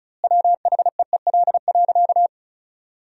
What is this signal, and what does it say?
Morse code, heard as a keyed tone, about 36 wpm, 700 Hz. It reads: WHEEL.
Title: 2x Word-Spacing - Sentences from Top 500 Words - 35wpm